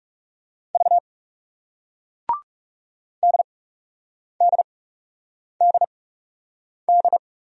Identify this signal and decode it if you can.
Morse code, heard as a keyed tone, about 38 wpm, spaced as for 8 wpm, 700 Hz. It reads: V DDDD